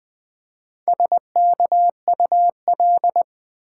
Morse code, heard as a keyed tone, about 20 wpm, 700 Hz. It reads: SKUL
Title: Build A Word - 1 Letter At A Time - 3 to 5-Letter Words - 20wpm